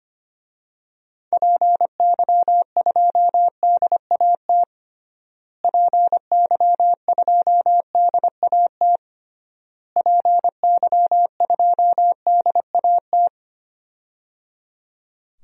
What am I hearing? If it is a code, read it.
Morse code, heard as a keyed tone, about 25 wpm, 700 Hz. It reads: PY2DAT PY2DAT PY2DAT
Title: DXCC-100 Call Signs - 6 Characters - Repeated 3x - 25wpm